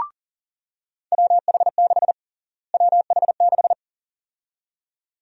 Morse code, heard as a keyed tone, about 40 wpm, 700 Hz. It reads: WH6 WH6
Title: USA Call Sign Prefixes - Repeated 2x - 40wpm